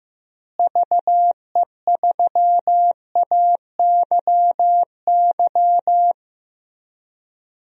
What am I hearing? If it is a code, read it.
Morse code, heard as a keyed tone, about 15 wpm, 700 Hz. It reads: VE3AYY
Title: DXCC-100 Call Signs - 6 Characters - Repeated 3x - 15wpm